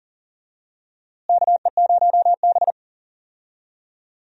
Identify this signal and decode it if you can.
Morse code, heard as a keyed tone, about 40 wpm, 700 Hz. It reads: KE0B
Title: USA Call Signs - 4 Characters - Repeated 3x - 40wpm